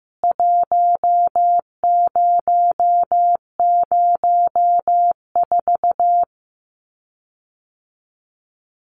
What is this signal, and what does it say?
Morse code, heard as a keyed tone, about 15 wpm, 700 Hz. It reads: 1004
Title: CWT Exchange 15wpm